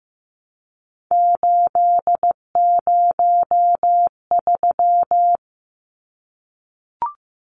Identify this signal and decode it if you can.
Morse code, heard as a keyed tone, about 15 wpm, 700 Hz. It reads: 803
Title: Numbers - 3-Digit 15wpm